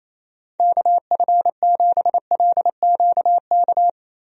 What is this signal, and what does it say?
Morse code, heard as a keyed tone, about 28 wpm, 700 Hz. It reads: KF7LQK
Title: USA Call Signs - 6 Characters - Repeated 3x - 28wpm